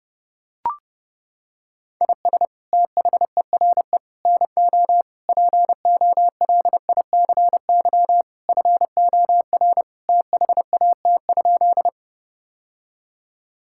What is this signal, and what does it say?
Morse code, heard as a keyed tone, about 30 wpm, 700 Hz. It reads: IS THERE NO POLICY FOR THAT?
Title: Sentences from Top 1000 Words - Review Part 5 - 30wpm